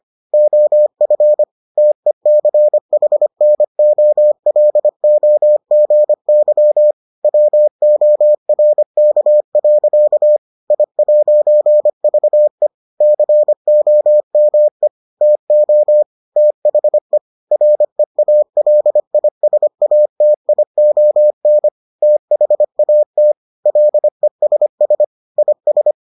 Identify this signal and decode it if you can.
Morse code, heard as a keyed tone, about 25 wpm, 600 Hz. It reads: OF TECHNOLOGY WORK. I'VE COME TO THE REALISATION THAT LESS IS